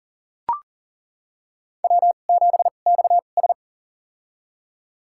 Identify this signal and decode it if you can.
Morse code, heard as a keyed tone, about 40 wpm, spaced as for 29 wpm, 700 Hz. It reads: W7XS
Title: USA Call Signs - 4 Characters - 2x Character Spacing - 40wpm